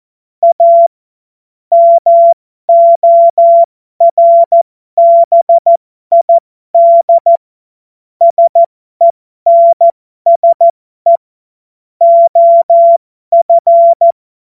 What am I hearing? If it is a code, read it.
Morse code, heard as a keyed tone, about 14 wpm, spaced as for 12 wpm, 680 Hz. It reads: A MORBID SENSE OF
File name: Morsekurs / Witze